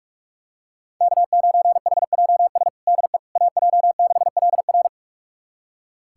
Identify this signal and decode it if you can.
Morse code, heard as a keyed tone, about 47 wpm, 700 Hz. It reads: K9HJS DE AJ6LR
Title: USA Callsign DE Callsign - 5 Characters - 45wpm